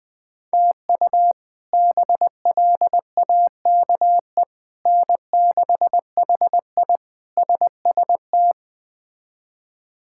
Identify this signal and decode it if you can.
Morse code, heard as a keyed tone, about 21 wpm, 700 Hz. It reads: TU BLAKE N6HI SST
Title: K1USN SST QSO Practice - 20wpm